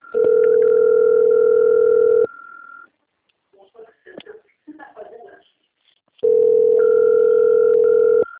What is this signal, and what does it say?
Ringback tone (North America), 440 + 480 Hz.